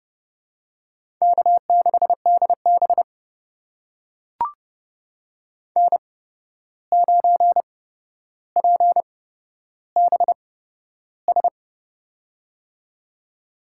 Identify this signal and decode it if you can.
Morse code, heard as a keyed tone, about 30 wpm, spaced as for 12 wpm, 700 Hz. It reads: K6DB N9PBS
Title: USA Call Signs - 8x Character Spacing - 30wpm